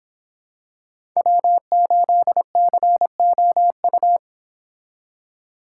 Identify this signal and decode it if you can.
Morse code, heard as a keyed tone, about 26 wpm, 700 Hz. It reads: W8COU